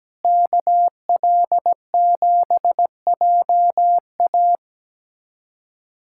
Morse code, heard as a keyed tone, about 17 wpm, 700 Hz. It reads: KL7JA
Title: USA Callsign DE Callsign - 5 Characters - 17wpm